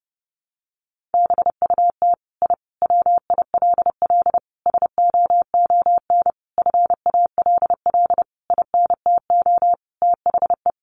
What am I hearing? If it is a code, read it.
Morse code, heard as a keyed tone, about 28 wpm, 700 Hz. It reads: BUT I WILL SOON FALL INTO THE
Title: Sentences from Top 1300 Words 30wpm